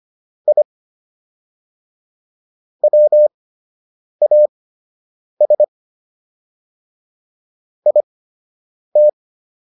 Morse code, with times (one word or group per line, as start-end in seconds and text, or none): 0.5-0.6 I
2.8-5.6 WAS
7.9-9.1 IT